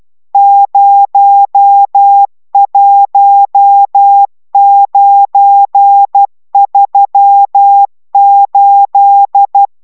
Morse code, heard as a keyed tone, about 12 wpm, 800 Hz. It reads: 01938